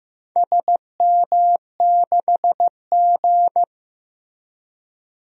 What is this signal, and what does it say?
Morse code, heard as a keyed tone, about 15 wpm, 700 Hz. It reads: SM6G
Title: DXCC-100 Call Signs - 4 Characters - Repeated 2x - 15wpm